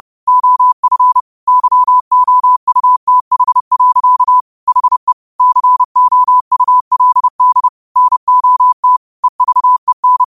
Morse code, heard as a keyed tone, about 30 wpm, 1000 Hz. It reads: OF YOUTH. HE COULD NOT EVEN